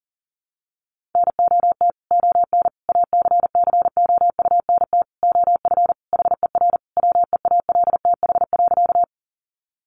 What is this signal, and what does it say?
Morse code, heard as a keyed tone, about 40 wpm, 700 Hz. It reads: NOT ON ACCOUNT OF HER WEALTH.
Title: Sentences from Top 1200 Words 40wpm